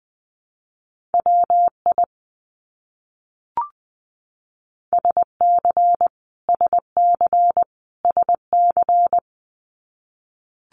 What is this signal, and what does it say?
Morse code, heard as a keyed tone, about 20 wpm, 700 Hz. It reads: WI SC SC SC
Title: US States Repeated 3x 20wpm